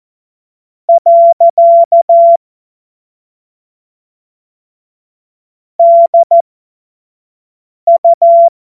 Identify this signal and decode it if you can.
Morse code, heard as a keyed tone, about 14 wpm, spaced as for 5 wpm, 680 Hz. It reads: . DU
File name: Morsekurs / Witze / Witz 21. September 2021, 25 F70